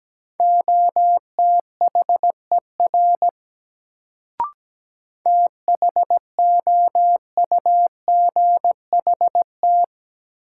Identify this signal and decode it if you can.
Morse code, heard as a keyed tone, about 17 wpm, 700 Hz. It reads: OTHER THOUGHT
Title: Sets of 2 Words - New Word Drill - 17wpm